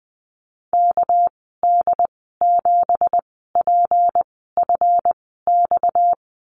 Morse code, heard as a keyed tone, about 20 wpm, spaced as for 15 wpm, 700 Hz. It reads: KD7PFX
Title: 245 Lesson 39g - Letter z - Call Signs 20wpm